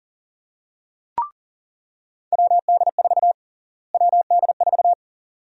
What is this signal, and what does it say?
Morse code, heard as a keyed tone, about 40 wpm, 700 Hz. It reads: WD4 WD4
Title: USA Call Sign Prefixes - Repeated 2x - 40wpm